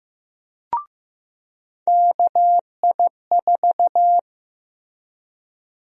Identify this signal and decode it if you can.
Morse code, heard as a keyed tone, about 15 wpm, 700 Hz. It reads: KI4
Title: USA Call Sign Prefixes - Repeated 2x - 15wpm